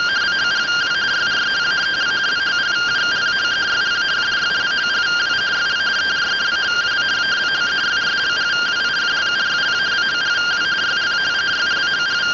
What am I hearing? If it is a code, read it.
Two-tone teleprinter signal, about 170 Hz shift.